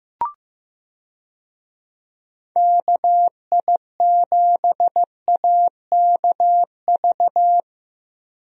Morse code, heard as a keyed tone, about 15 wpm, 700 Hz. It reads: KI7AKV
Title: USA Call Signs - 6 Characters - Repeated 2x - 15wpm